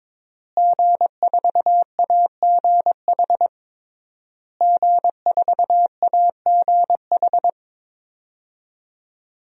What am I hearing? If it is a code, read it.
Morse code, heard as a keyed tone, about 22 wpm, 700 Hz. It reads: G4AGH G4AGH
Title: DXCC-100 Call Signs - 5 Characters - Repeated 3x - 22wpm